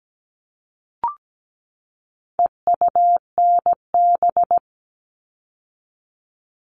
Morse code, heard as a keyed tone, about 17 wpm, 700 Hz. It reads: EUNB
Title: All Cut Numbers - 4-Digit 17wpm